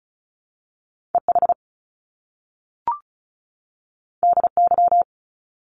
Morse code, heard as a keyed tone, about 36 wpm, 700 Hz. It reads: EH DY
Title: Top Two Letter Combinations in Words 35wpm